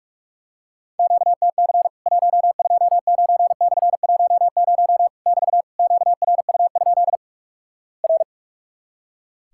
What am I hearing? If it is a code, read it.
Morse code, heard as a keyed tone, about 44 wpm, 700 Hz. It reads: QTC 129/10 <BT> QRU? R
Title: Worked All Europe DX Contest - QTC Exchange - 45wpm